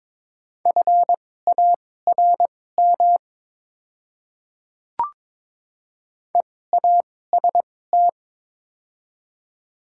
Morse code, heard as a keyed tone, about 22 wpm, spaced as for 16 wpm, 700 Hz. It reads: FARM EAST
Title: 2x Character Spacing - Top 500 Words - 22wpm